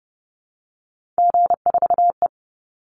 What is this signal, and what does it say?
Morse code, heard as a keyed tone, about 30 wpm, 700 Hz. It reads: G4E
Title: Call Signs - 30wpm